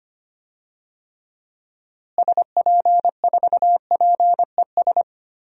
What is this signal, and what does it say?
Morse code, heard as a keyed tone, about 25 wpm, 700 Hz. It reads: SP4PES
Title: DXCC-100 Call Signs - 6 Characters - Repeated 2x - 25wpm